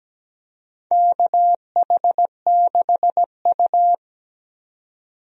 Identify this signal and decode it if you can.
Morse code, heard as a keyed tone, about 17 wpm, 700 Hz. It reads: KH6U
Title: DXCC-100 Call Signs - 4 Characters - 17wpm